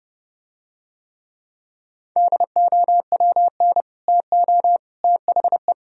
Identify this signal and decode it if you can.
Morse code, heard as a keyed tone, about 29 wpm, 700 Hz. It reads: DOWN TO THE